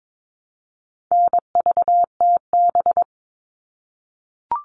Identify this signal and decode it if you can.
Morse code, heard as a keyed tone, about 22 wpm, 700 Hz. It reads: NVTB
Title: All Cut Numbers - 4-Digit 22wpm